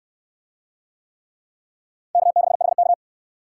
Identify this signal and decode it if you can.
Morse code, heard as a keyed tone, about 69 wpm, 700 Hz. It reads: K6HK